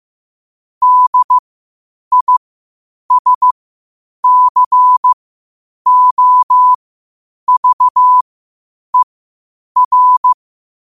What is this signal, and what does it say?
Morse code, heard as a keyed tone, about 15 wpm, spaced as for 8 wpm, 1000 Hz. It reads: DISCOVER
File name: qotdfiles / 8wpm